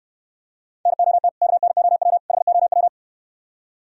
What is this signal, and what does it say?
Morse code, heard as a keyed tone, about 68 wpm, 700 Hz. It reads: NOT KNOW HOW